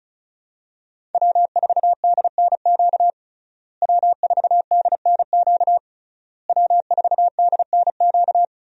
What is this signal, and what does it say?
Morse code, heard as a keyed tone, about 36 wpm, 700 Hz. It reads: W4DNQ W4DNQ W4DNQ
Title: USA Call Signs - 5 Characters - Repeated 3x - 35wpm